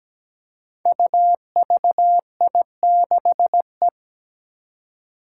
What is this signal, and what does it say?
Morse code, heard as a keyed tone, about 17 wpm, 700 Hz. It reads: UVI6E
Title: Code Groups - 5 Letter:Numbers - 17wpm